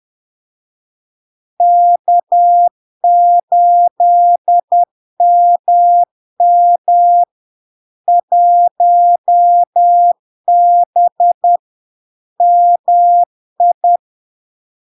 Morse code, heard as a keyed tone, about 10 wpm, 700 Hz. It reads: K8MM 1B MI
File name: LARK-FD-2022-Log-in-CW-10wpm-700-Hz